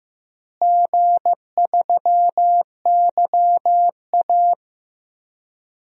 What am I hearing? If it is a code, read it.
Morse code, heard as a keyed tone, about 15 wpm, 700 Hz. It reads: G3YA
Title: DXCC-100 Call Signs - 4 Characters - Repeated 3x - 15wpm